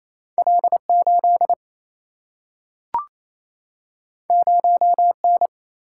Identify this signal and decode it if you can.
Morse code, heard as a keyed tone, about 28 wpm, 700 Hz. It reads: L 8 0 N
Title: Single-Letter Single-Number 28wpm